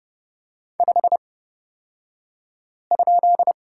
Morse code, heard as a keyed tone, about 30 wpm, spaced as for 5 wpm, 700 Hz. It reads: 5?
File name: course / 30:10wpm / 235 Lesson 38d - Letter q - Two Characters 30wpm